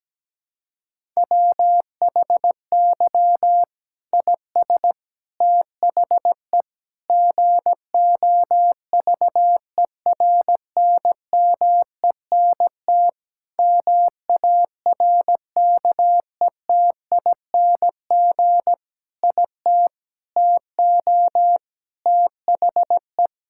Morse code, heard as a keyed tone, about 17 wpm, 700 Hz. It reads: WHY IS THE GOVERNMENT MARKETING IT TO THE
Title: Sentences from Top 1000 Words - Review Part 5 - 17wpm